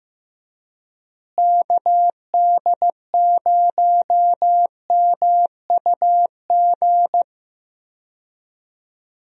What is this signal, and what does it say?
Morse code, heard as a keyed tone, about 15 wpm, 700 Hz. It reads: KD0MUG
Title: USA Call Signs - 6 Characters - Repeated 3x - 15wpm